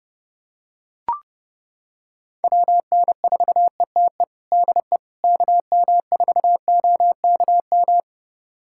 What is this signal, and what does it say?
Morse code, heard as a keyed tone, about 29 wpm, 700 Hz. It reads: WN4ETE DE KM4OKM
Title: USA Callsign DE Callsign 30wpm